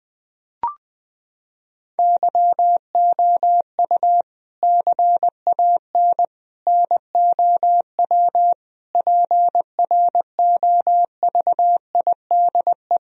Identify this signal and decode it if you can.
Morse code, heard as a keyed tone, about 20 wpm, 700 Hz. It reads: YOU CAN NOW PROVIDE
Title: Sentences from Top 1000 Words - Review Part 3 - 20wpm